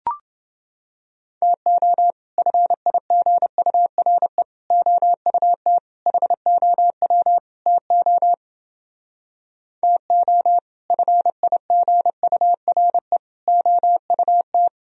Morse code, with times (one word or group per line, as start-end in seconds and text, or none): 1.4-2.1 TO
2.4-4.4 FIGURE
4.7-5.8 OUT
6.1-7.4 HOW
7.7-8.3 TO
9.8-10.6 TO
10.9-13.2 FIGURE
13.5-14.7 OUT